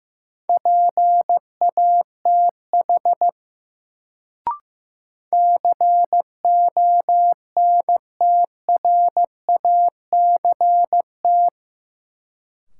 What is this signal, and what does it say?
Morse code, heard as a keyed tone, about 15 wpm, 700 Hz. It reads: PATH CONTRACT
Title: Sentences from Top 1400 Words - New Word Drill Alt - 15wpm